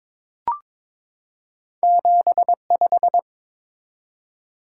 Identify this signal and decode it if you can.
Morse code, heard as a keyed tone, about 22 wpm, 700 Hz. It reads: 75
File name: practice / 22wpm / Two Letter-Number Combinations - 22wpm